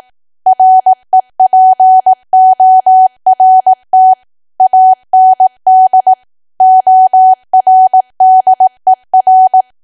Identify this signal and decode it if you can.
Morse code, heard as a keyed tone, about 18 wpm, 750 Hz. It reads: REPORT AND ORDER
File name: cwa / W1AW-Level2 / 18WPM